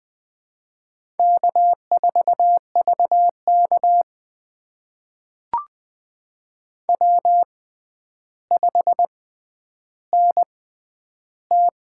Morse code, heard as a keyed tone, about 20 wpm, spaced as for 10 wpm, 700 Hz. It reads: K4VK W5NT
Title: USA Call Signs - 4 Characters - 6x Character Spacing - 20wpm